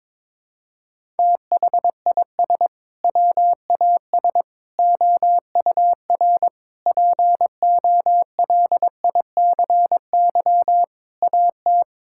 Morse code, heard as a keyed tone, about 22 wpm, 700 Hz. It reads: THIS WAS OUR POLICY AT
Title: Sentences from Top 1000 Words - Review Part 5 - 22wpm